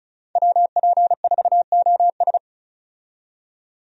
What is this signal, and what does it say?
Morse code, heard as a keyed tone, about 34 wpm, 700 Hz. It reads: WP4OS
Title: USA Call Signs - 5 Characters - 35wpm